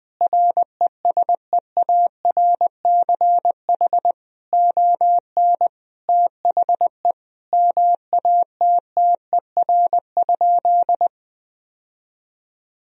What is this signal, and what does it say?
Morse code, heard as a keyed tone, about 20 wpm, 700 Hz. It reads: RESEARCH ON THE MATTER?
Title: Sentences from Top 1000 Words - Review Part 4 - 20wpm